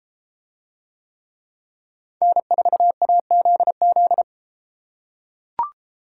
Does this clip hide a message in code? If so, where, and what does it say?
Yes: Morse code from 2.2 to 4.2 s, heard as a keyed tone, about 30 wpm, 700 Hz. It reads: N4AZZ